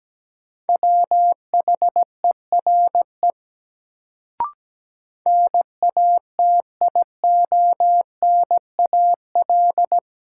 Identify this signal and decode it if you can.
Morse code, heard as a keyed tone, about 17 wpm, 700 Hz. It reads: WHERE NATIONAL
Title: Sets of 2 Words - New Word Drill - 17wpm